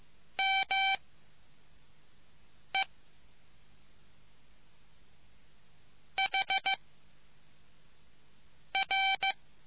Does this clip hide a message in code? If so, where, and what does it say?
Yes: Morse code from 0.4 to 9.3 s, heard as a keyed tone, about 15 wpm, 780 Hz. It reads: M E H R